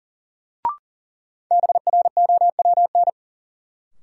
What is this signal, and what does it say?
Morse code, heard as a keyed tone, about 40 wpm, 700 Hz. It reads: BROWN
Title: Sentences from Top 1200 Words - New Word Drill - 40wpm